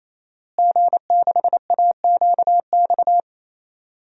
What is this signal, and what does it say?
Morse code, heard as a keyed tone, about 28 wpm, 700 Hz. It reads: G6AQX
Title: DXCC-100 Call Signs - 5 Characters - Repeated 3x - 28wpm